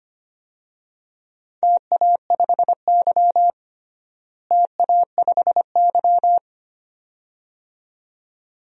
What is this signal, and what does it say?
Morse code, heard as a keyed tone, about 25 wpm, 700 Hz. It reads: TA5Y TA5Y
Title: DXCC-100 Call Signs - 4 Characters - Repeated 2x - 25wpm